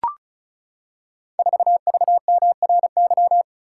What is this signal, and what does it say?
Morse code, heard as a keyed tone, about 36 wpm, 700 Hz. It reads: 4VMRY
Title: Code Groups - 5 Letter:Numbers - 35wpm